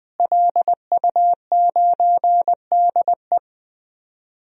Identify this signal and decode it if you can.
Morse code, heard as a keyed tone, about 21 wpm, 700 Hz. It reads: LU9DE
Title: DXCC-100 Call Signs - 5 Characters - Repeated 3x - 20wpm